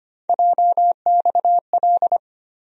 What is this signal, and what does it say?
Morse code, heard as a keyed tone, about 25 wpm, 700 Hz. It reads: JXL
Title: USA Call Sign Postfixes - 3 Letters 25wpm